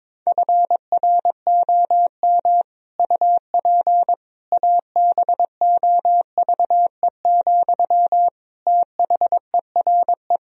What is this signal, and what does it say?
Morse code, heard as a keyed tone, about 22 wpm, 700 Hz. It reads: FROM UP ABOVE, THERE
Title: Sentences from Top 1000 Words - Review Part 2 - 22wpm